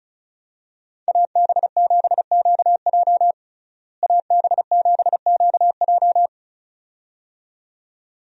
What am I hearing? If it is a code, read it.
Morse code, heard as a keyed tone, about 36 wpm, 700 Hz. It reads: AB7QJ AB7QJ
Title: USA Call Signs - 5 Characters - Repeated 2x - 35wpm